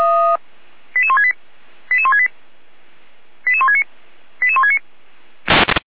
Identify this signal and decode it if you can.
Five-tone selective call (ZVEI), heard as quick runs of five tones, tones of about 70 ms. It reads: 88168 88168 88169 88169
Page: /455/SDR/ZVEI/